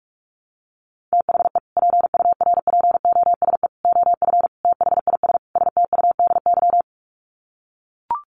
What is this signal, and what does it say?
Morse code, heard as a keyed tone, about 44 wpm, 700 Hz. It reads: THE PURPOSE OF THIS STUDY
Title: Sets of 5 Words 45wpm